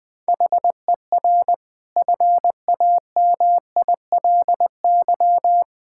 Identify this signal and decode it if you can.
Morse code, heard as a keyed tone, about 20 wpm, 700 Hz. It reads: HER FAMILY
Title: Sentences from Top 1000 Words - Review Part 6 - 20wpm